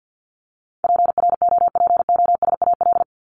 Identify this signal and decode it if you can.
Morse code, heard as a keyed tone, about 48 wpm, 700 Hz. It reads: PROPOSAL
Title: Top 1751-2000 Words 50wpm